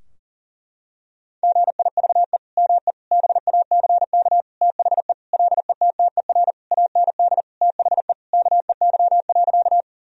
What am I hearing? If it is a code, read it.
Morse code, heard as a keyed tone, about 40 wpm, 700 Hz. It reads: GIVE ME BACK THE LETTER AND THE KEY.
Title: Sentences from Top 1000 Words - Review Part 4 - 40wpm